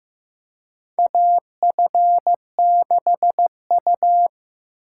Morse code, heard as a keyed tone, about 15 wpm, 700 Hz. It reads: AF6U
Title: USA Call Signs - 4 Characters - 3x Character Spacing - 15wpm